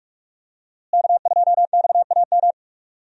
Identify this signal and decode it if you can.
Morse code, heard as a keyed tone, about 46 wpm, 700 Hz. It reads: K2XAM
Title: USA Call Signs - 5 Characters - Repeated 3x - 45wpm